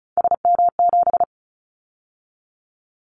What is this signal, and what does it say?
Morse code, heard as a keyed tone, about 36 wpm, 700 Hz. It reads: SM7
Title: Callsign Prefixes for Top 25 DXCC - 35wpm